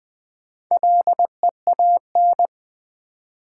Morse code, heard as a keyed tone, about 20 wpm, 700 Hz. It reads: LEAN
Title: Build A Word - 1 Letter At A Time - 4-Letter Words - 20wpm